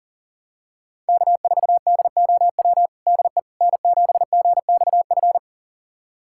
Morse code, heard as a keyed tone, about 40 wpm, 700 Hz. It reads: K4DOW DE N7GXF
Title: USA Callsign DE Callsign - 5 Characters - 40wpm